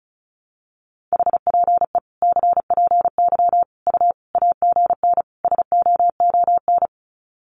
Morse code, heard as a keyed tone, about 34 wpm, 700 Hz. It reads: HPE CPY U AGN SOON
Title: Common QSO Phrases 35wpm